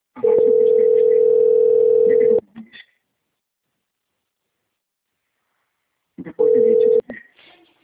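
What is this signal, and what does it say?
Ringback tone (North America), 440 + 480 Hz.
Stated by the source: Outgoing phone call recording, Nova Odessa - SP.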